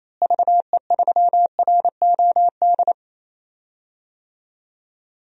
Morse code, heard as a keyed tone, about 28 wpm, 700 Hz. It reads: VE3ROD
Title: DXCC-100 Call Signs - 6 Characters - Repeated 3x - 28wpm